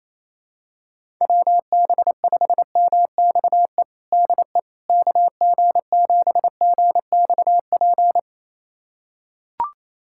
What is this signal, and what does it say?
Morse code, heard as a keyed tone, about 28 wpm, 700 Hz. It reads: WB5MXE DE KG7GXP
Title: USA Callsign DE Callsign - 6 Characters - 28wpm